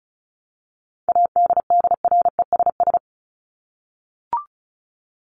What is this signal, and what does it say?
Morse code, heard as a keyed tone, about 34 wpm, 700 Hz. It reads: ADDRESS
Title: Top 751-1000 Words 35wpm